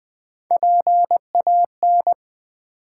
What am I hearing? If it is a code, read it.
Morse code, heard as a keyed tone, about 20 wpm, 700 Hz. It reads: PAN
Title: Build A Word - 1 Letter At A Time - 3-Letter Words - 20wpm